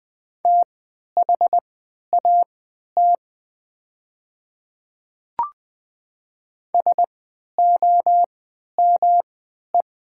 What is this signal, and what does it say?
Morse code, heard as a keyed tone, about 20 wpm, spaced as for 11 wpm, 700 Hz. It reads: THAT SOME
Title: 3x Character Spacing - Top 200 Words - 20wpm